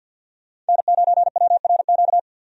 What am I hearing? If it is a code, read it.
Morse code, heard as a keyed tone, about 52 wpm, 700 Hz. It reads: N9WRQ
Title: USA Call Signs - 5 Characters - 50wpm